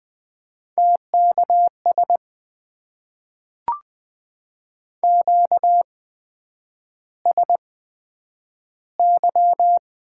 Morse code, heard as a keyed tone, about 20 wpm, spaced as for 9 wpm, 700 Hz. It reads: TKS QSY
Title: Top 90 QSO Elements - 8x Character Spacing - 20wpm